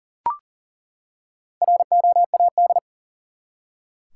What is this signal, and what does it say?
Morse code, heard as a keyed tone, about 40 wpm, 700 Hz. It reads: ROAD
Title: Four Letter Words - 40wpm